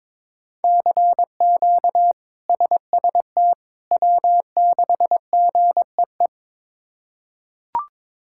Morse code, heard as a keyed tone, about 21 wpm, 700 Hz. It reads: CQ SST W6GEE
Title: K1USN SST QSO Practice - 22wpm